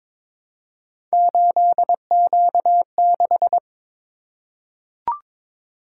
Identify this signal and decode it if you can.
Morse code, heard as a keyed tone, about 22 wpm, 700 Hz. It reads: 8Q6
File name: practice / 22wpm / Three Letter-Number Combinations - 22wpm